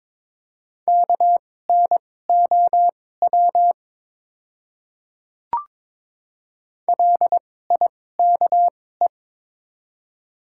Morse code, heard as a keyed tone, about 22 wpm, spaced as for 16 wpm, 700 Hz. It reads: KNOW LIKE
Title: 2x Character Spacing - Top 100 Words - 22wpm